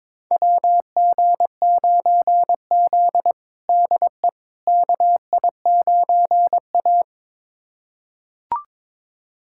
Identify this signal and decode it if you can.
Morse code, heard as a keyed tone, about 22 wpm, 700 Hz. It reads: WG9Z DE KI9A
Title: USA Callsign DE Callsign - 4 Characters - 22wpm